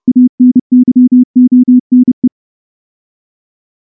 Morse code, heard as a keyed tone, about 28 wpm, 250 Hz. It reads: ANYONE
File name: morse-training-30WPM